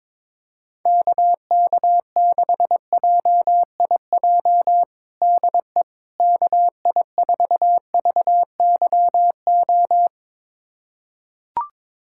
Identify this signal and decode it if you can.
Morse code, heard as a keyed tone, about 22 wpm, 700 Hz. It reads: KK6JIJ DE KI4VYO
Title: USA Callsign DE Callsign - 6 Characters - 22wpm